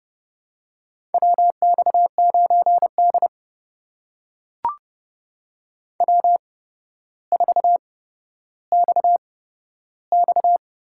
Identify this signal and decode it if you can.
Morse code, heard as a keyed tone, about 32 wpm, spaced as for 13 wpm, 700 Hz. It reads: WX9D W4XX
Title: USA Call Signs - 4 Characters - 8x Character Spacing - 30wpm